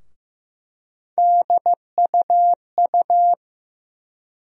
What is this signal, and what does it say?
Morse code, heard as a keyed tone, about 15 wpm, 700 Hz. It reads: DUU
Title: USA Call Sign Postfixes - 3 Letters 15wpm